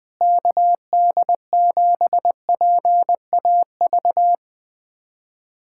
Morse code, heard as a keyed tone, about 20 wpm, 700 Hz. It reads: KD7PAV
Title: USA Callsign DE Callsign 20wpm